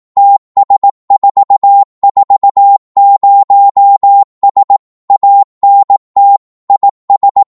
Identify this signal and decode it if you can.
Morse code, heard as a keyed tone, about 18 wpm, 800 Hz. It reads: TS440S ANT IS
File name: Morse / qso-18wpm